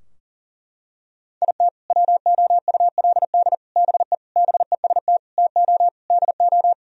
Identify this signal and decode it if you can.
Morse code, heard as a keyed tone, about 37 wpm, 700 Hz. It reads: IT WOULD BE BEST TO DO
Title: Sentences from Top 1000 Words - Review Part 2 - 40wpm